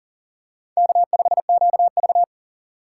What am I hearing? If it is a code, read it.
Morse code, heard as a keyed tone, about 40 wpm, 700 Hz. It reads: K5QV
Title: USA Call Signs - 6x Character Spacing - 40wpm